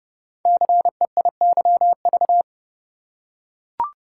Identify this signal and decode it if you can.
Morse code, heard as a keyed tone, about 30 wpm, 700 Hz. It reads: CEIYV
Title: Code Groups - 5 Letter:Numbers - 30wpm